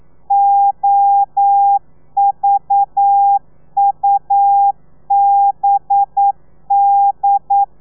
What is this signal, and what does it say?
Morse code, heard as a keyed tone, about 9 wpm, 800 Hz. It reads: OVUBD
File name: telegrafia / i2tfj / Lez4 40